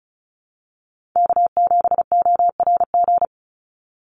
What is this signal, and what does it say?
Morse code, heard as a keyed tone, about 34 wpm, 700 Hz. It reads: K7ORG
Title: NA SOTA Chaser Callsigns 35wpm